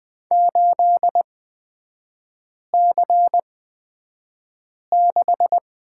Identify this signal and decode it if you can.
Morse code, heard as a keyed tone, about 20 wpm, spaced as for 5 wpm, 700 Hz. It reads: 8C6
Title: 250 Lesson 40e - BK - Three Characters 20wpm